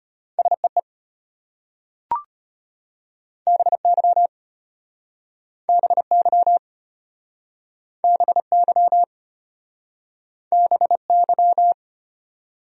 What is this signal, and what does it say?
Morse code, heard as a keyed tone, about 32 wpm, 700 Hz. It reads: SEE BY BY BY BY